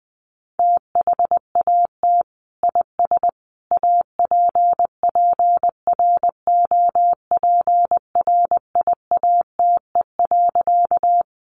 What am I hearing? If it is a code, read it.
Morse code, heard as a keyed tone, about 20 wpm, 700 Hz. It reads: THAT IS APPROPRIATE.
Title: Sentences from Top 900 Words 20wpm